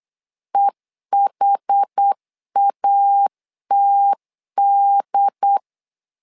Morse code, heard as a keyed tone, about 8 wpm, 800 Hz. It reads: EHATD
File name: LEZIONI / 5 LBVC / 5 LB 8wpm 2816